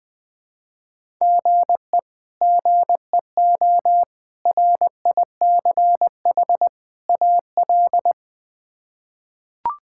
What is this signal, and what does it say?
Morse code, heard as a keyed tone, about 20 wpm, 700 Hz. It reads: GE GEO RICH AL
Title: K1USN SST QSO Practice - 20wpm